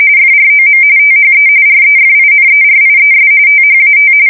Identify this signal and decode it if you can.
RTTY, about 45.45 baud. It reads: FROM THE REFINERIES THAT
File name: fsk170 / usnavy4